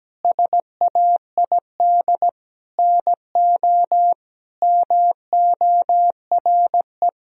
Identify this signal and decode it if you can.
Morse code, heard as a keyed tone, about 17 wpm, 700 Hz. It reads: SAID NO MORE
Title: Sentences from Top 1000 Words - Review Part 1 - 17wpm